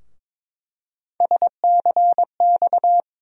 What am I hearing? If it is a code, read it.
Morse code, heard as a keyed tone, about 22 wpm, 700 Hz. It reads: SCX
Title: USA Call Sign Postfixes - 3 Letters 22wpm